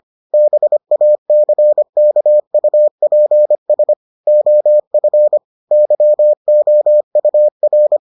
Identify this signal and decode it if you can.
Morse code, heard as a keyed tone, about 25 wpm, 600 Hz. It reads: BACKUPS OF YOUR